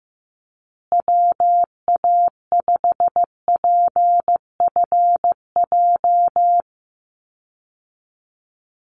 Morse code, heard as a keyed tone, about 15 wpm, 700 Hz. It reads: WA5PFJ
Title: Call Signs Repeated 2x - 15wpm